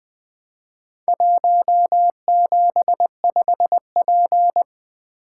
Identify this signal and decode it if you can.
Morse code, heard as a keyed tone, about 23 wpm, 700 Hz. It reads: 175P
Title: Speed-Racing - Four Letter-Number Combinations - 20wpm